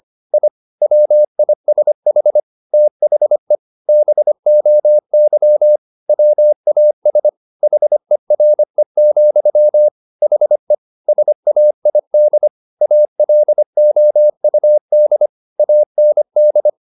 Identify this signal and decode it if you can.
Morse code, heard as a keyed tone, about 25 wpm, 600 Hz. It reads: I WISH THE BOY WAS HERE, HE SAID ALOUD AND
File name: the old man and the sea part 1 25wpm